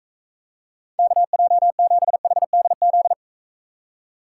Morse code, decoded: KJ7HDZ

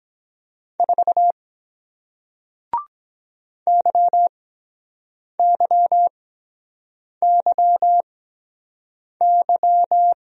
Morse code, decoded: 4 YYYY

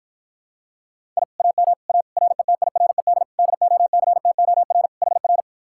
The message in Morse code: I AM A RETIRED DOCTOR HR

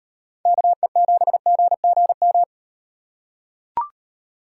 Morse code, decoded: KE7GGM

